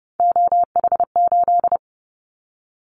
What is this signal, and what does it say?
Morse code, heard as a keyed tone, about 30 wpm, 700 Hz. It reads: OH8
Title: Callsign Prefixes for Top 25 DXCC - 30wpm